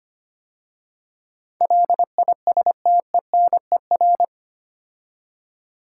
Morse code, heard as a keyed tone, about 25 wpm, 700 Hz. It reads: LISTENER
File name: 8 Letter Words - First 3 Letters Given - 25wpm